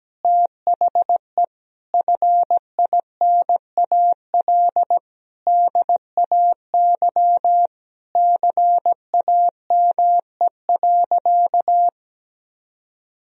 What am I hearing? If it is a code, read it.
Morse code, heard as a keyed tone, about 17 wpm, 700 Hz. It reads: THE FINAL DAY CAME.
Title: Sentences from Top 1000 Words - Review Part 2 - 17wpm